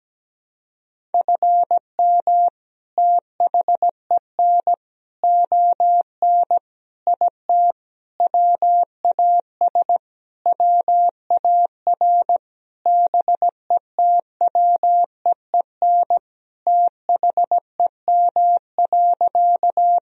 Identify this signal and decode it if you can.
Morse code, heard as a keyed tone, about 17 wpm, 700 Hz. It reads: FM THEN ON IT WAS WAR BETWEEN THEM.
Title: Sentences from Top 400 Words and Abbreviations 17wpm